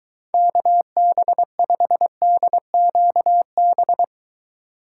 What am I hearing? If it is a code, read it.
Morse code, heard as a keyed tone, about 23 wpm, 700 Hz. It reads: KB5DQB